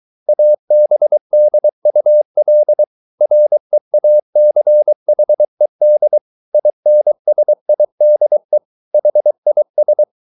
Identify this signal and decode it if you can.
Morse code, heard as a keyed tone, about 23 wpm, 600 Hz. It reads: ABDUL REACHED INSIDE HIS